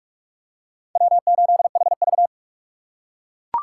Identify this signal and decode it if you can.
Morse code, heard as a keyed tone, about 41 wpm, 700 Hz. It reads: W8HV